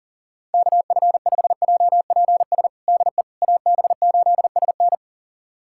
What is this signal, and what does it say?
Morse code, heard as a keyed tone, about 40 wpm, 700 Hz. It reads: KF5JPS DE AB8SN
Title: USA Callsign DE Callsign 40wpm